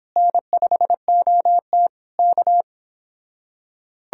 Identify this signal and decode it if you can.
Morse code, heard as a keyed tone, about 26 wpm, 700 Hz. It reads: N5OT K